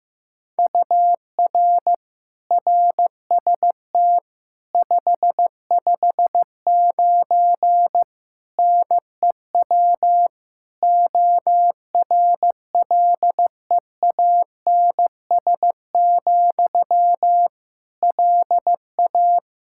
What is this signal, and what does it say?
Morse code, heard as a keyed tone, about 15 wpm, 700 Hz. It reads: UR RST 559 NEW ORLEANS, LA
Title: Practice QSO - Exchange 1 - Beginning - 15wpm